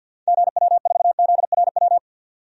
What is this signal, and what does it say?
Morse code, heard as a keyed tone, about 48 wpm, 700 Hz. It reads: GW4ZRW
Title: DXCC-100 Call Signs - 6 Characters - 50wpm